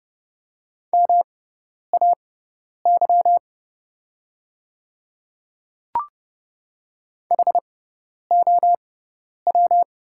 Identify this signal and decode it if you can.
Morse code, heard as a keyed tone, about 31 wpm, spaced as for 10 wpm, 700 Hz. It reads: MAY HOW
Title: 6x Character Spacing - Top 100 Words - 30wpm